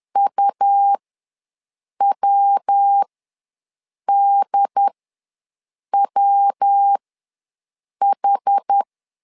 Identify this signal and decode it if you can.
Morse code, heard as a keyed tone, about 10 wpm, spaced as for 6 wpm, 800 Hz. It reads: UWDWH
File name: LEZIONI / 3 MWRO / 3 MW ws 10wpm 2213